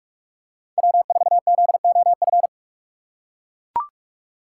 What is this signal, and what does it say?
Morse code, heard as a keyed tone, about 44 wpm, 700 Hz. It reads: W4ZOF
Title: Code Groups - 5 Letter:Numbers - 45wpm